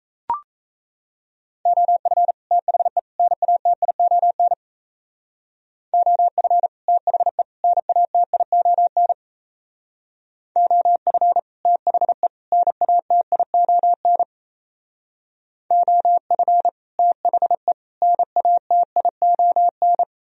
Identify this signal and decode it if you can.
Morse code, heard as a keyed tone, about 35 wpm, 700 Hz. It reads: OF THE NATION OF THE NATION OF THE NATION OF THE NATION